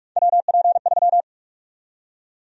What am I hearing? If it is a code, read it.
Morse code, heard as a keyed tone, about 46 wpm, 700 Hz. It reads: WP3
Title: USA Call Sign Prefixes - 45wpm